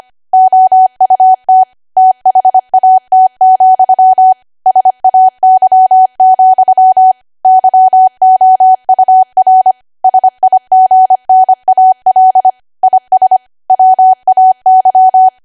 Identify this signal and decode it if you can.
Morse code, heard as a keyed tone, about 25 wpm, 750 Hz. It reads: OUT THAT, SAY, YOUR SIGNAL IS WAY
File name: Archive / 25 WPM / 25WPM